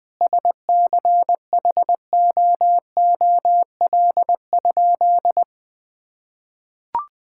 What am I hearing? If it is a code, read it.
Morse code, heard as a keyed tone, about 21 wpm, 700 Hz. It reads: SCHOOL?
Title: Sentences from Top 1000 Words - Review Part 5 - 20wpm